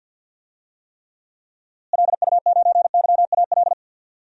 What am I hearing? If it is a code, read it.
Morse code, heard as a keyed tone, about 49 wpm, 700 Hz. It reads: LU9YAL
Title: DXCC-100 Call Signs - 6 Characters - Repeated 3x - 50wpm